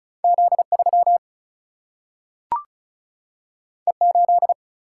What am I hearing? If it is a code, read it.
Morse code, heard as a keyed tone, about 36 wpm, 700 Hz. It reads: Z3 E8
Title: Single-Letter Single-Number 35wpm